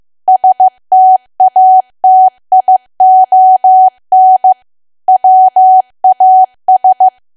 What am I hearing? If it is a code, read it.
Morse code, heard as a keyed tone, about 15 wpm, 750 Hz. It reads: STATION WAS
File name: cwa / W1AW-Level2 / 15WPM